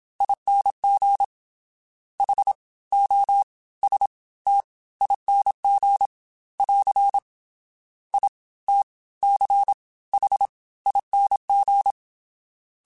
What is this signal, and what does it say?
Morse code, heard as a keyed tone, about 27 wpm, spaced as for 19 wpm, 800 Hz. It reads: ING HOSTING <AR> ITCHING